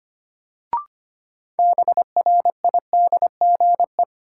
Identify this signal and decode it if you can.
Morse code, heard as a keyed tone, about 25 wpm, 700 Hz. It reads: BRIDGE
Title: Sentences from Top 1200 Words - New Word Drill Alt - 25wpm